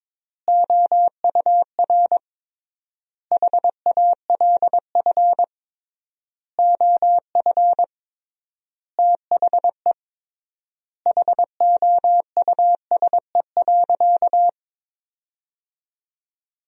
Morse code, heard as a keyed tone, about 22 wpm, 700 Hz. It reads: OUR HALF OF THE HOUSE.
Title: 3x Word-Spacing - Sentences from Top 400 Words - 22wpm